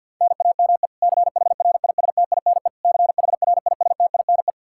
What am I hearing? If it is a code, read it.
Morse code, heard as a keyed tone, about 46 wpm, 700 Hz. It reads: NAME CHRISTINE CHRISTINE
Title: Practice QSO - Exchange 1 - Beginning - 50wpm